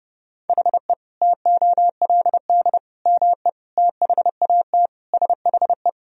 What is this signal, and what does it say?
Morse code, heard as a keyed tone, about 30 wpm, 700 Hz. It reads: HE TOLD ME THAT SHE
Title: Sentences from Top 1000 Words - Review Part 2 - 30wpm